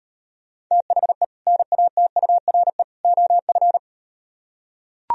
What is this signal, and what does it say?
Morse code, heard as a keyed tone, about 38 wpm, 700 Hz. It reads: THE NATURE OF